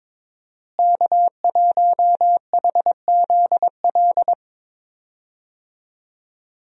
Morse code, heard as a keyed tone, about 21 wpm, 700 Hz. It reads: K1HZL
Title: USA Call Signs - 5 Characters - Repeated 3x - 22wpm